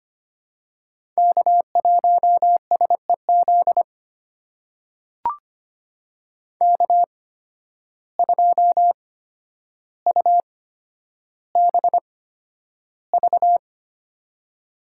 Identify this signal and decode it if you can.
Morse code, heard as a keyed tone, about 25 wpm, spaced as for 11 wpm, 700 Hz. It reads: K1SEZ K2UBV
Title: USA Call Signs - 5 Characters - 8x Character Spacing - 25wpm